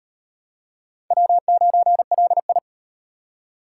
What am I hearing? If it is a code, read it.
Morse code, heard as a keyed tone, about 38 wpm, 700 Hz. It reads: W9LI